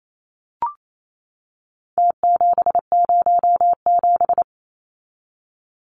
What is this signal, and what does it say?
Morse code, heard as a keyed tone, about 28 wpm, 700 Hz. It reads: T707
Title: Numbers - Contest Style - 4-Digit 28wpm